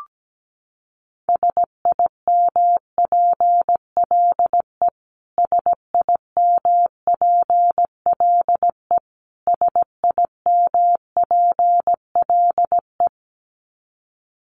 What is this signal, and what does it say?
Morse code, heard as a keyed tone, about 17 wpm, 700 Hz. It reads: SIMPLE SIMPLE SIMPLE
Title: Top 500 Words Repeated 17wpm